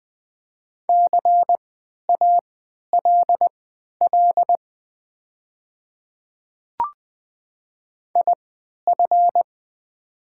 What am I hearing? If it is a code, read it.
Morse code, heard as a keyed tone, about 20 wpm, spaced as for 11 wpm, 700 Hz. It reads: CALL IF
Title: 3x Character Spacing - Top 100 Words - 20wpm